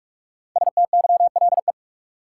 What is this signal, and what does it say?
Morse code, heard as a keyed tone, about 46 wpm, 700 Hz. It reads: STYLE
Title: Sentences from Top 800 Words - New Word Drill - 45wpm